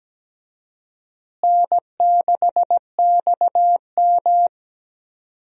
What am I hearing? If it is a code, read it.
Morse code, heard as a keyed tone, about 17 wpm, 700 Hz. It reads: N6XM
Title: USA Call Signs - 4 Characters - Repeated 2x - 17wpm